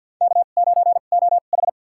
Morse code, heard as a keyed tone, about 52 wpm, spaced as for 36 wpm, 700 Hz. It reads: K9OH
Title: USA Call Signs - 2x Character Spacing - 50wpm